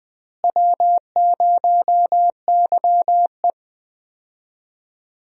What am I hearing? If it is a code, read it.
Morse code, heard as a keyed tone, about 20 wpm, 700 Hz. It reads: W0YE
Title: USA Call Signs - 4 Characters - Repeated 3x - 20wpm